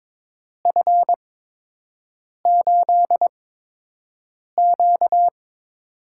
Morse code, heard as a keyed tone, about 21 wpm, spaced as for 6 wpm, 700 Hz. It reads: F8Q
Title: USA Call Signs - 8x Character Spacing - 22wpm